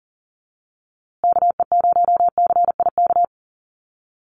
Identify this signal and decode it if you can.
Morse code, heard as a keyed tone, about 40 wpm, 700 Hz. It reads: KE0CIK
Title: USA Call Signs Repeated 3x 40wpm